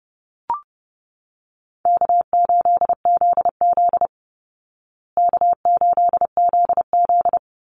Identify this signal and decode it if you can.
Morse code, heard as a keyed tone, about 30 wpm, 700 Hz. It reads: K8ZZ K8ZZ
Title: Call Signs Repeated 3x - 30wpm